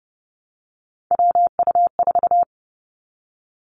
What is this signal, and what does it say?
Morse code, heard as a keyed tone, about 32 wpm, 700 Hz. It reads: WU4
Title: Three Challenging Letter-Number Combinations 30wpm